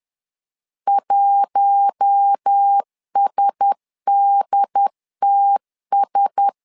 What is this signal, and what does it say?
Morse code, heard as a keyed tone, about 10 wpm, 800 Hz. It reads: 1SDTS
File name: LEZIONI / 4 Q1S 10wpm 2133